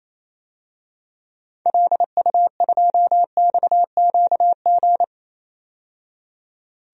Morse code, heard as a keyed tone, about 28 wpm, 700 Hz. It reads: LU2XQG
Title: DXCC-100 Call Signs - 6 Characters - 28wpm